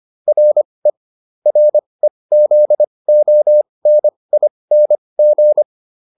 Morse code, heard as a keyed tone, about 25 wpm, spaced as for 20 wpm, 600 Hz. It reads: RE REZONING